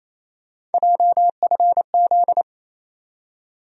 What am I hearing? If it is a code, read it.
Morse code, heard as a keyed tone, about 28 wpm, 700 Hz. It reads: JFZ
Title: USA Call Sign Postfixes - 3 Letters 28wpm